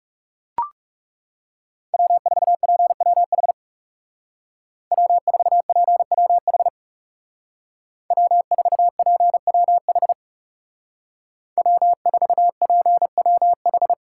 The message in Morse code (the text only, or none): W4PWH W4PWH W4PWH W4PWH